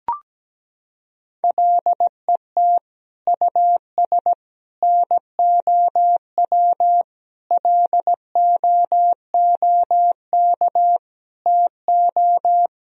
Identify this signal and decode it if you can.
Morse code, heard as a keyed tone, about 17 wpm, 700 Hz. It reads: LET US NOW LOOK TO
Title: Sentences from Top 1000 Words - Review Part 1 - 17wpm